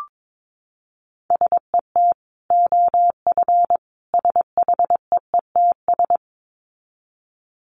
Morse code, heard as a keyed tone, about 22 wpm, 700 Hz. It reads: SET OF SHEETS
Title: Dit Challenge 22wpm